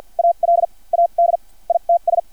Morse code, heard as a keyed tone, about 50 wpm, spaced as for 32 wpm, 680 Hz. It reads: ALADITS
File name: archived / CRSnet